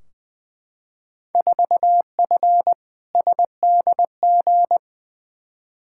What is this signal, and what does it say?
Morse code, heard as a keyed tone, about 20 wpm, 700 Hz. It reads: 4F SDG
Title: Field Day Exchanges - 20wpm